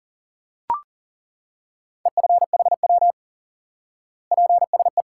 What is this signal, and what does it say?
Morse code, heard as a keyed tone, about 40 wpm, 700 Hz. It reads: EFHW PSE
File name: IQE Course 3 - Lesson 89 - YAGI - Two Elements - 40wpm